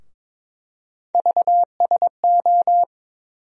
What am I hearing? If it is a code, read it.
Morse code, heard as a keyed tone, about 21 wpm, 700 Hz. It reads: VSO
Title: USA Call Sign Postfixes - 3 Letters 22wpm